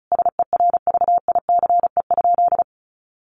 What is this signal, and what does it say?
Morse code, heard as a keyed tone, about 34 wpm, 700 Hz. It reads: SERVICE?
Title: Sentences from Top 1500 Words 35wpm